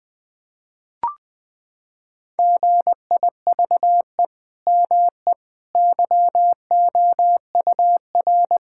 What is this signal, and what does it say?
Morse code, heard as a keyed tone, about 20 wpm, 700 Hz. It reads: GIVE ME YOUR
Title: Sentences from Top 1000 Words - Review Part 6 - 20wpm